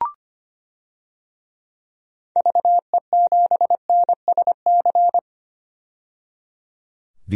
Morse code, heard as a keyed tone, about 25 wpm, 700 Hz. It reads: VE7NSC
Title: DXCC-100 Call Signs - 6 Characters - 25wpm